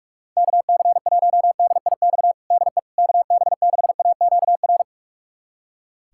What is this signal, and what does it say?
Morse code, heard as a keyed tone, about 46 wpm, 700 Hz. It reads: KC1DIX DE KB6AQR
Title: USA Callsign DE Callsign 45wpm